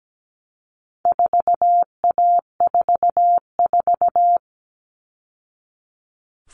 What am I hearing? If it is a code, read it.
Morse code, heard as a keyed tone, about 17 wpm, 700 Hz. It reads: 4A44
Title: All Cut Numbers - 4-Digit 17wpm